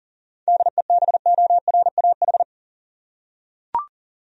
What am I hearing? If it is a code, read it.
Morse code, heard as a keyed tone, about 37 wpm, 700 Hz. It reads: DEBORAH
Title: USA Common Names - 40wpm